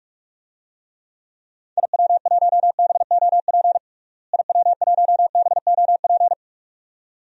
Morse code, heard as a keyed tone, about 44 wpm, 700 Hz. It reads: IW1BOP IW1BOP
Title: DXCC-100 Call Signs - 6 Characters - Repeated 2x - 45wpm